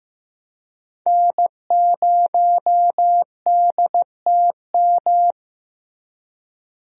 Morse code, heard as a keyed tone, about 16 wpm, 700 Hz. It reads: N0DTM
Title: USA Call Signs - 5 Characters - Repeated 3x - 15wpm